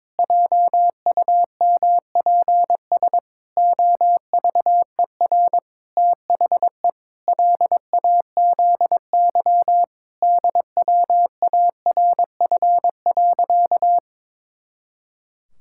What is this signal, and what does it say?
Morse code, heard as a keyed tone, about 22 wpm, 700 Hz. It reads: JUMPS OVER THE LAZY DWARF.
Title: Pangram Challenge v2 - 22wpm